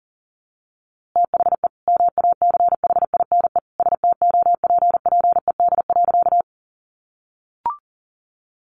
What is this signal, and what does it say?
Morse code, heard as a keyed tone, about 40 wpm, 700 Hz. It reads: THE MACHINE STOPPED.
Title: A Sound Of Thunder 40wpm